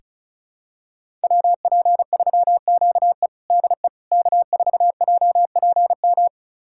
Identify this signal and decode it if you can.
Morse code, heard as a keyed tone, about 36 wpm, 700 Hz. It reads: WP3QE DE K4JPM
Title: USA Callsign DE Callsign - 5 Characters - 35wpm